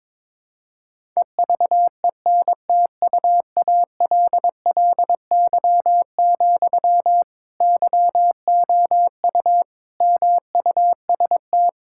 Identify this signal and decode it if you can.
Morse code, heard as a keyed tone, about 22 wpm, 700 Hz. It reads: EVENTUALLY, YOU MUST
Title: Sentences from Top 1000 Words - Review Part 6 - 22wpm